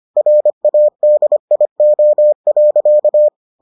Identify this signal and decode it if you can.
Morse code, heard as a keyed tone, about 25 wpm, 600 Hz. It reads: RADIO.